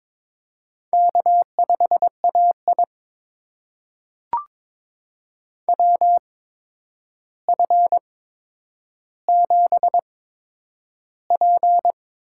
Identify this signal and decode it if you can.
Morse code, heard as a keyed tone, about 21 wpm, spaced as for 9 wpm, 700 Hz. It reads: K5AI WF7P
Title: USA Call Signs - 4 Characters - 8x Character Spacing - 22wpm